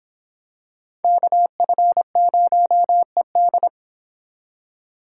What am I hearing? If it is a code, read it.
Morse code, heard as a keyed tone, about 26 wpm, 700 Hz. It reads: KF0ED